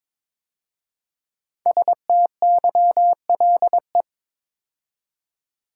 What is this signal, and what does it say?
Morse code, heard as a keyed tone, about 22 wpm, 700 Hz. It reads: STYLE
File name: 5 Letter Words - First 3 Letters Given - 22wpm